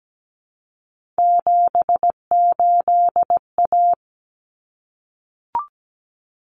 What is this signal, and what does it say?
Morse code, heard as a keyed tone, about 17 wpm, 700 Hz. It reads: 78A
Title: Common Cut Numbers - 3-Digit 17wpm